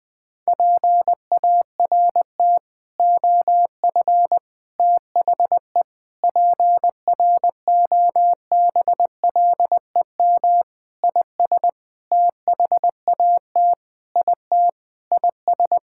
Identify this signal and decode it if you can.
Morse code, heard as a keyed tone, about 21 wpm, 700 Hz. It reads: PART OF THE PROBLEM IS THAT IT IS
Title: Sentences with 6-Word Phrases - 20wpm